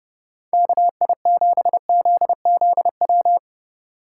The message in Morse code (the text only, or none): KI7ZZW